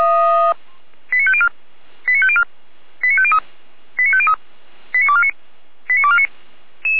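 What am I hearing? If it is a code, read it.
Five-tone selective call (ZVEI), heard as quick runs of five tones, tones of about 70 ms. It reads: 89504 89504 89503 89503 89249 89249
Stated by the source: ZVEI five-tone selective call received on an SDR by the fire brigade feuerwehr-orlamuende.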